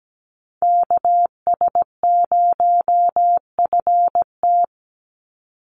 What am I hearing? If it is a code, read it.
Morse code, heard as a keyed tone, about 17 wpm, 700 Hz. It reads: KS0FT
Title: USA Call Signs 17wpm